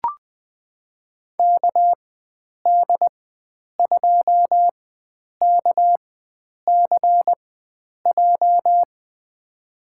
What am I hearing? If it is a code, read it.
Morse code, heard as a keyed tone, about 20 wpm, spaced as for 9 wpm, 700 Hz. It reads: KD2KCJ
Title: USA Call Signs - 4x Character Spacing - 20wpm